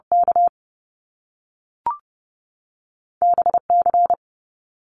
Morse code, heard as a keyed tone, about 30 wpm, 700 Hz. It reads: K BC
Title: Canadian Provinces and Territories 30wpm